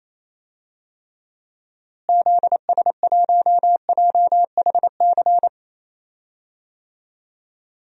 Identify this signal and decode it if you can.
Morse code, heard as a keyed tone, about 28 wpm, 700 Hz. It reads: ZS1JHC
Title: DXCC-100 Call Signs - 6 Characters - Repeated 3x - 28wpm